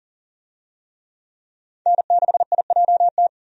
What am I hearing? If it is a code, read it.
Morse code, heard as a keyed tone, about 40 wpm, 700 Hz. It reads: N6IJT
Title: USA Call Signs - 5 Characters - 40wpm